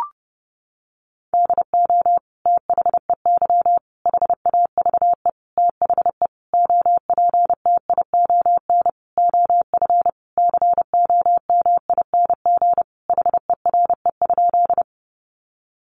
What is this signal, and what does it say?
Morse code, heard as a keyed tone, about 30 wpm, 700 Hz. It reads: DO THEY HAVE THE OPTION OF COMING HERE?
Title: Sentences from Top 1300 Words 30wpm